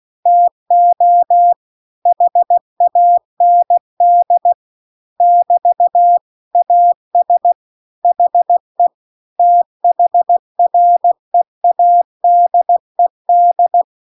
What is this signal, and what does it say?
Morse code, heard as a keyed tone, about 16 wpm, 700 Hz. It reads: TO HAND <BT> AS HE THREADED
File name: CW-The-Big-Twoheaded-River-2-16WPM